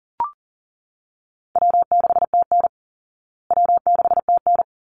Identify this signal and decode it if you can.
Morse code, heard as a keyed tone, about 40 wpm, 700 Hz. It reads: W6TN W6TN
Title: Easy Call Signs Repeated 2x - 40wpm